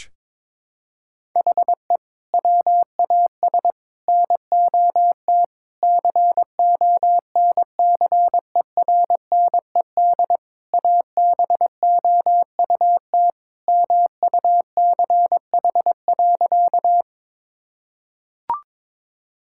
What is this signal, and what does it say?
Morse code, heard as a keyed tone, about 22 wpm, 700 Hz. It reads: HE WAS NOT CONCERNED ABOUT MUCH.
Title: Sentences from Top 1000 Words - Review Part 6 - 22wpm